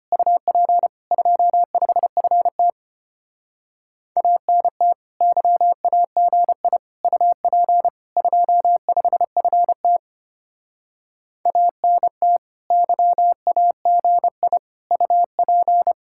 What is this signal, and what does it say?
Morse code, heard as a keyed tone, about 30 wpm, 700 Hz. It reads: UP 25FT ANT YAGI UP 25FT ANT YAGI UP
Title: Speed-Racing Common QSO Phrases Encore - 25wpm